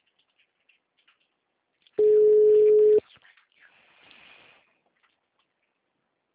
Ringback tone (Europe, 425 Hz).
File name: out- -2011